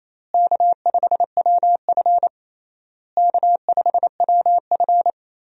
Morse code, heard as a keyed tone, about 28 wpm, 700 Hz. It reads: K5WF K5WF
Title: USA Call Signs - 4 Characters - Repeated 3x - 28wpm